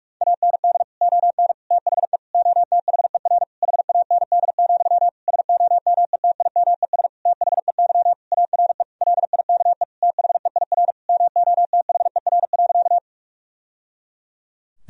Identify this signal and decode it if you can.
Morse code, heard as a keyed tone, about 46 wpm, 700 Hz. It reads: AND ON THE OTHER HAND, SOMETIMES THEY ARE LIKE THEIR MOTHER.
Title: Sentences from Top 1000 Words - Review Part 3 - 45wpm